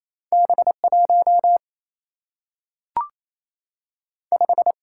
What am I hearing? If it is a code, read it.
Morse code, heard as a keyed tone, about 28 wpm, 700 Hz. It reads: B1 5
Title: Single-Letter Single-Number 28wpm